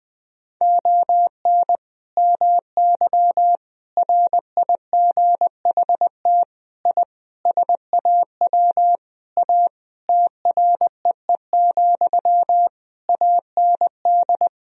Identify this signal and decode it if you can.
Morse code, heard as a keyed tone, about 20 wpm, 700 Hz. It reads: ON MY RIGHT I SAW A TREE, AND
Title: Sentences from Top 1000 Words - Review Part 2 - 20wpm